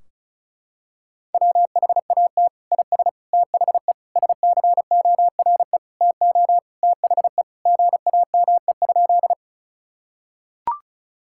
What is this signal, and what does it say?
Morse code, heard as a keyed tone, about 34 wpm, 700 Hz. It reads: WHAT IS THE SCORE TO THE GAME?
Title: Sentences from Top 1600 Words 35wpm